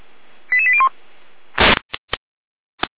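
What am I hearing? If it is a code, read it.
Five-tone selective call (ZVEI), heard as quick runs of five tones, tones of about 70 ms. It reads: 88801
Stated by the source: ZVEI five-tone selective call received via SDR, from the feuerwehr-orlamuende.org server (Feuerwehr Orlamünde, fire brigade).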